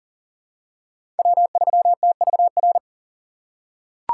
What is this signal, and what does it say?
Morse code, heard as a keyed tone, about 38 wpm, 700 Hz. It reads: W3TVR
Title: USA Call Signs - 5 Characters - 3x Character Spacing - 40wpm